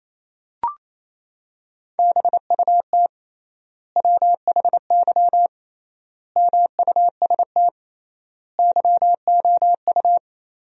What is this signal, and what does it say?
Morse code, heard as a keyed tone, about 28 wpm, 700 Hz. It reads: BUT WHY MUST YOU
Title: 3x Word-Spacing - Sentences from Top 400 Words - 28wpm